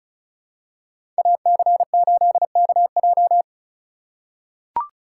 Morse code, heard as a keyed tone, about 36 wpm, 700 Hz. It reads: AC8KJ
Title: USA Call Signs - 2x Character Spacing - 35wpm